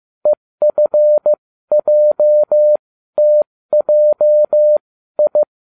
Morse code, heard as a keyed tone, about 15 wpm, spaced as for 13 wpm, 600 Hz. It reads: EFJTJI